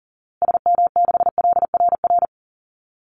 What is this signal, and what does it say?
Morse code, heard as a keyed tone, about 40 wpm, 700 Hz. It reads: SM6LRR
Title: Call Signs Repeated 2x - 40wpm